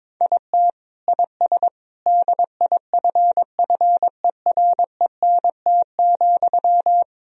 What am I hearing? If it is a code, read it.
Morse code, heard as a keyed tone, about 22 wpm, 700 Hz. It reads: IT IS DIFFERENT,
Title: Sentences from Top 1000 Words - Review Part 3 - 22wpm